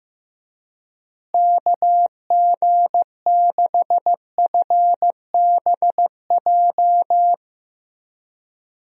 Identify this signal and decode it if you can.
Morse code, heard as a keyed tone, about 15 wpm, 700 Hz. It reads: KG6FBJ
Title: USA Call Signs - 6 Characters - Repeated 2x - 15wpm